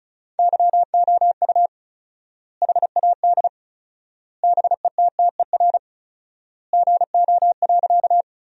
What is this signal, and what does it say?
Morse code, heard as a keyed tone, about 36 wpm, 700 Hz. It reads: YOU HAD BETTER GO.
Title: 4x Word-Spacing - Sentences from Top 400 Words - 35wpm